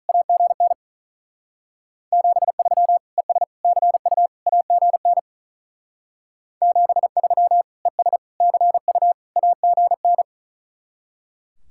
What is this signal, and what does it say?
Morse code, heard as a keyed tone, about 46 wpm, 700 Hz. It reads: AGN 73 ES CU AGN 73 ES CU AGN